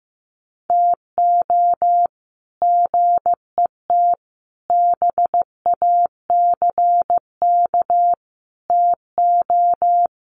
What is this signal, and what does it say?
Morse code, heard as a keyed tone, about 15 wpm, 700 Hz. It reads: TO GET BACK TO
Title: Sets of 5 Words Encore 15wpm